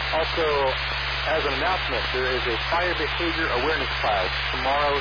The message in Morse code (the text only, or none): none